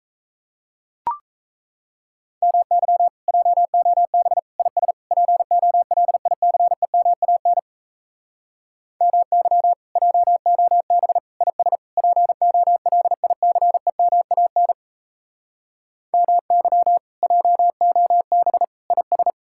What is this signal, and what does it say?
Morse code, heard as a keyed tone, about 38 wpm, 700 Hz. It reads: MY JOB IS POLICEMAN MY JOB IS POLICEMAN MY JOB IS